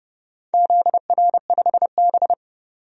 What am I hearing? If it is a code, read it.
Morse code, heard as a keyed tone, about 31 wpm, 700 Hz. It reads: ZR5B
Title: DXCC-100 Call Signs - 4 Characters - 30wpm